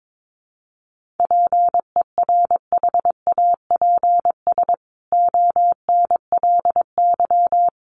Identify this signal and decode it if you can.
Morse code, heard as a keyed tone, about 22 wpm, 700 Hz. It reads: PERHAPS ONLY
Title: A Sound Of Thunder 22wpm